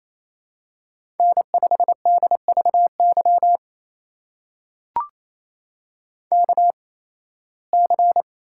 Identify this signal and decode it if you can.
Morse code, heard as a keyed tone, about 28 wpm, spaced as for 18 wpm, 700 Hz. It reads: N5DVY KC
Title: USA Call Signs - 8x Character Spacing - 28wpm